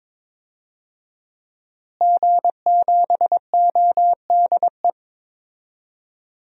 Morse code, heard as a keyed tone, about 22 wpm, 700 Hz. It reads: G7ODE
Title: DXCC-100 Call Signs - 5 Characters - Repeated 2x - 22wpm